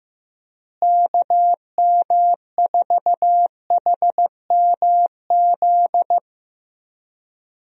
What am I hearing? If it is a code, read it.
Morse code, heard as a keyed tone, about 15 wpm, 700 Hz. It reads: KM4HMZ
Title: USA Call Signs - 6 Characters - Repeated 3x - 15wpm